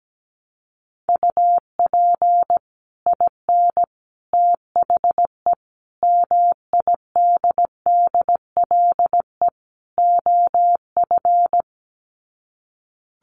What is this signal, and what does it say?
Morse code, heard as a keyed tone, about 17 wpm, 700 Hz. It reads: UP IN THE MIDDLE OF
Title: Sets of 5 Words Encore 17wpm